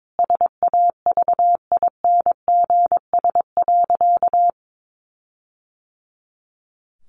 Morse code, heard as a keyed tone, about 22 wpm, 700 Hz. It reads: SAVINGS.
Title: Sentences from Top 1400 Words 22wpm